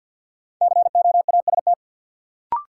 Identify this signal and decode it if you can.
Morse code, heard as a keyed tone, about 53 wpm, 700 Hz. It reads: COAST
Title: Sentences from Top 1000 Words - Review Part 4 - New Word Drill - 50wpm